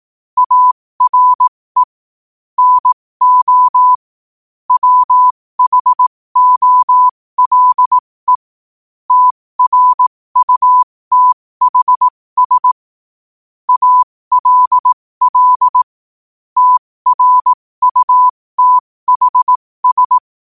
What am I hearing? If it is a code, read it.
Morse code, heard as a keyed tone, about 18 wpm, spaced as for 15 wpm, 1000 Hz. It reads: ARE NO WHOLE TRUTHS ALL TRUTHS
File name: qotdfiles / 15wpm